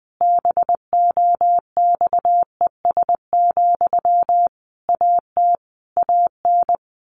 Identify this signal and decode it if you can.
Morse code, heard as a keyed tone, about 20 wpm, 700 Hz. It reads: BOXES, AT AN
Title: A Sound Of Thunder 20wpm